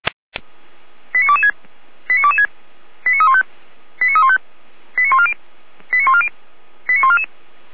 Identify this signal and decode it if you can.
Five-tone selective call (ZVEI), heard as quick runs of five tones, tones of about 80 ms. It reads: 89226 89226 89315 89315 89140 89140 89144